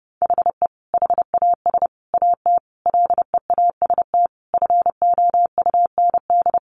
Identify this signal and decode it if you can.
Morse code, heard as a keyed tone, about 30 wpm, 700 Hz. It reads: HE HAS AT LEAST FOUND
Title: Sentences from Top 1500 Words 30wpm